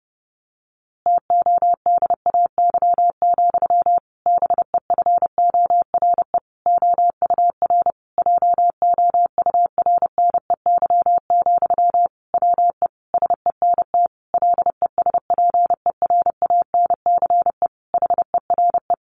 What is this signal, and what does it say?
Morse code, heard as a keyed tone, about 30 wpm, 700 Hz. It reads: TODAY, BEFORE OUR JOURNEY, WE SENT LESPERANCE HERE
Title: A Sound Of Thunder 30wpm